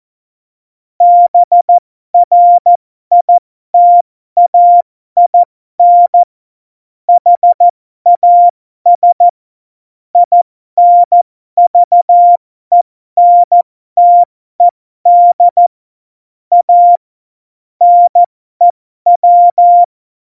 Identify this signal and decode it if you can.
Morse code, heard as a keyed tone, about 14 wpm, spaced as for 12 wpm, 700 Hz. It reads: BRITAIN HAS INVENTED A NEW
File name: Morsekurs / Witze